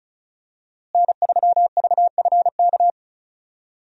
Morse code, heard as a keyed tone, about 34 wpm, 700 Hz. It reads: N3VFK
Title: USA Call Signs - 5 Characters - Repeated 2x - 35wpm